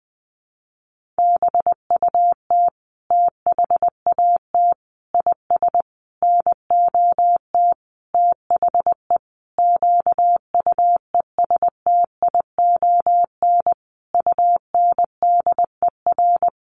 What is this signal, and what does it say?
Morse code, heard as a keyed tone, about 19 wpm, 700 Hz. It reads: BUT THAT IS NOT THE QUESTION UNDER
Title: Sentences from Top 1500 Words 20wpm